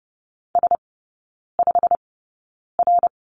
Morse code, heard as a keyed tone, about 30 wpm, spaced as for 9 wpm, 700 Hz. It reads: S5R
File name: course / 30:10wpm / 087 Lesson 15e - Number 5 - Three Characters 30wpm